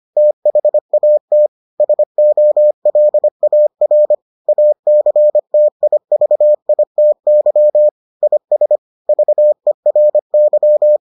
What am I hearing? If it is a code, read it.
Morse code, heard as a keyed tone, about 25 wpm, 600 Hz. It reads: THAT SOLAR ACTIVITY IS VERY